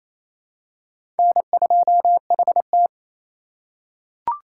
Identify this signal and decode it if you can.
Morse code, heard as a keyed tone, about 27 wpm, 700 Hz. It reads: N2HT
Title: USA Call Signs - 4 Characters - 4x Character Spacing - 28wpm